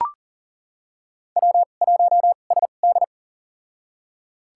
Morse code, heard as a keyed tone, about 40 wpm, spaced as for 29 wpm, 700 Hz. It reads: W1SD
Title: USA Call Signs - 4 Characters - 2x Character Spacing - 40wpm